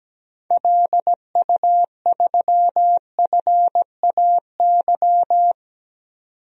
Morse code, heard as a keyed tone, about 17 wpm, 700 Hz. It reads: LU3FAY
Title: DXCC-100 Call Signs - 6 Characters - 17wpm